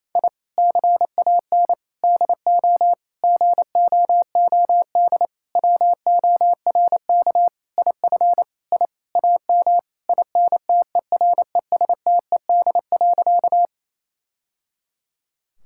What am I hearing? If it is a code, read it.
Morse code, heard as a keyed tone, about 28 wpm, 700 Hz. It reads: I CAN DO GOOD WORK IF I AM INTERESTED.
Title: Sentences from Top 1000 Words - Review Part 6 - 28wpm